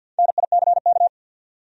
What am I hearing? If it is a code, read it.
Morse code, heard as a keyed tone, about 52 wpm, 700 Hz. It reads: NICK
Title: USA Common Names - Max 5 Letters - 50wpm